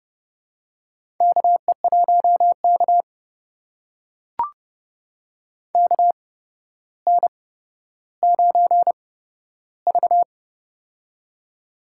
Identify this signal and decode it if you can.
Morse code, heard as a keyed tone, about 31 wpm, spaced as for 13 wpm, 700 Hz. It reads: KE1K KN9V
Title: USA Call Signs - 4 Characters - 8x Character Spacing - 30wpm